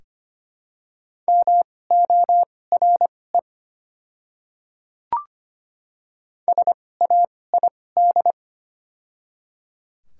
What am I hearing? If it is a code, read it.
Morse code, heard as a keyed tone, about 25 wpm, spaced as for 18 wpm, 700 Hz. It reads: MORE SAID
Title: 2x Character Spacing - Top 300 Words - 25wpm